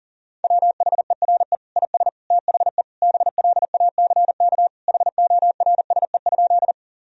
Morse code, heard as a keyed tone, about 40 wpm, 700 Hz. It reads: WHERE IS THE BLACK HORSE?
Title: Sentences from Top 1000 Words - Review Part 2 - 40wpm